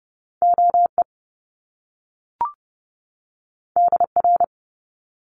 Morse code, heard as a keyed tone, about 30 wpm, 700 Hz. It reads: OE DR
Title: Top Two Letter Combinations in Words 30wpm